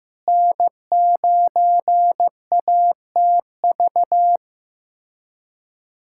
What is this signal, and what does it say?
Morse code, heard as a keyed tone, about 15 wpm, 700 Hz. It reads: N9ATV
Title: USA Callsign DE Callsign - 5 Characters - 15wpm